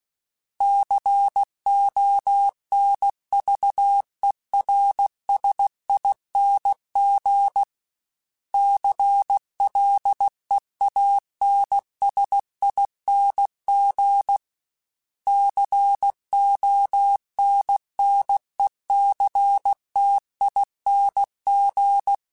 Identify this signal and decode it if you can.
Morse code, heard as a keyed tone, about 16 wpm, 800 Hz. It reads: CONVERSING CLEANSING CONNECTING
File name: Special INGs / Special INGs-mp3s / Sngl ing @15